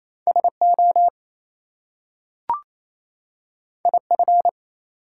Morse code, heard as a keyed tone, about 29 wpm, 700 Hz. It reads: SO IF
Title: Two Letter Words - 28wpm